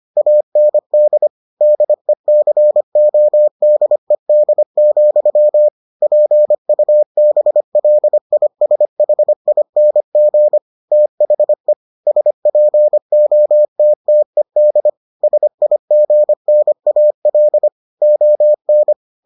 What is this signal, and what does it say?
Morse code, heard as a keyed tone, about 25 wpm, 600 Hz. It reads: AND DECODED, PUBLISHING THE SPOTTED SIGNAL ON